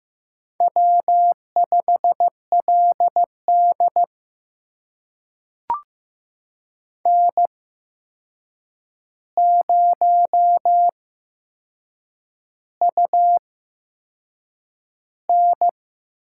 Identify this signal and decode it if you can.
Morse code, heard as a keyed tone, about 15 wpm, spaced as for 6 wpm, 700 Hz. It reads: W5LD N0UN
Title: USA Call Signs - 4 Characters - 8x Character Spacing - 15wpm